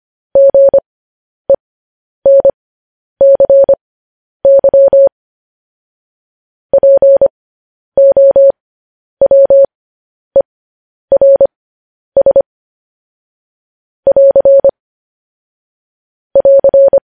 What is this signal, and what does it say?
Morse code, heard as a keyed tone, about 25 wpm, spaced as for 10 wpm, 550 Hz. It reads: GENCY POWERS <AR> <AR>
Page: news headlines in morse code at 10 wpm